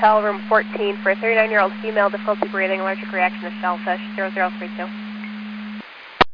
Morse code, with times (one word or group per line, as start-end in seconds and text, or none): none